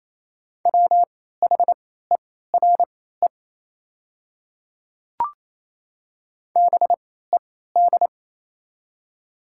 Morse code, heard as a keyed tone, about 28 wpm, spaced as for 16 wpm, 700 Hz. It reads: WHERE BED
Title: 3x Character Spacing - Top 500 Words - 28wpm